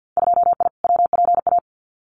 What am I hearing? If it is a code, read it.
Morse code, heard as a keyed tone, about 50 wpm, 700 Hz. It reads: 2I WPA
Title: Winter Field Day Exchanges 50wpm